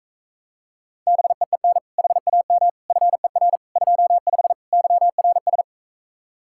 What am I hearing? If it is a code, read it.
Morse code, heard as a keyed tone, about 45 wpm, 700 Hz. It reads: BEEN HAM FER 25 YRS